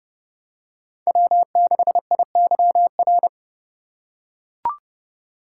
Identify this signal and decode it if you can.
Morse code, heard as a keyed tone, about 29 wpm, 700 Hz. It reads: W6IYR